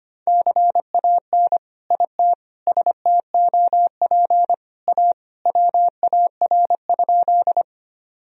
Morse code, heard as a keyed tone, about 25 wpm, 700 Hz. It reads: CAN IT STOP A WAR?
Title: Sentences from Top 1000 Words - Review Part 2 - 25wpm